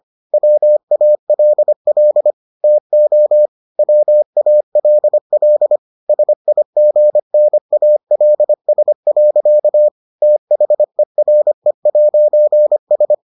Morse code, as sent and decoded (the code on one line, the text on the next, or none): .-- .- .-.. .-.. / - --- / .-- .- .-.. .-.. / ... .. --. -. .- .-.. ... .-.-.- / - .... .-. .----. ...
WALL TO WALL SIGNALS. THERE'S